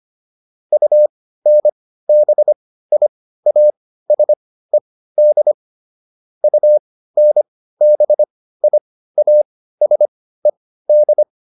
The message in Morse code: UNBIASED UNBIASED

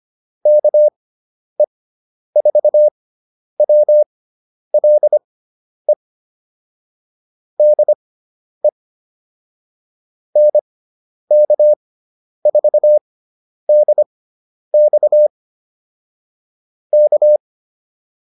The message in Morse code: KE4WLE DE NK4DX K

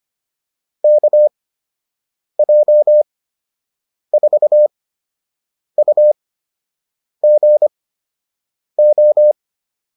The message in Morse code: KJ4UGO

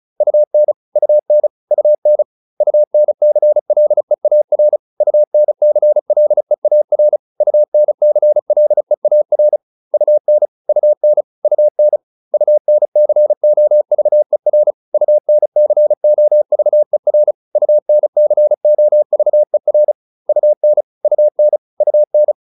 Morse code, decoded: UN UN UN UNCLEAR UNCLEAR UNCLEAR UN UN UN UNCOVER UNCOVER UNCOVER UN UN UN

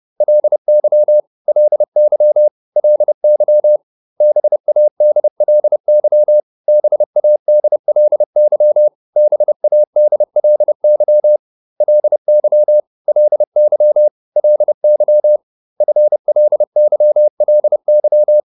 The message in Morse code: LY LY LY BADLY BADLY BADLY LY LY LY FLYLY

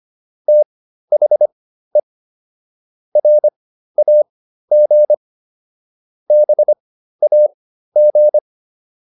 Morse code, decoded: THE RAG BAG

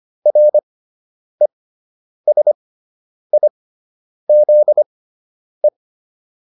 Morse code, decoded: RESIZE

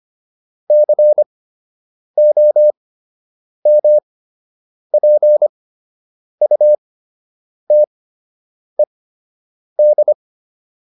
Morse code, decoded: COMPUTED

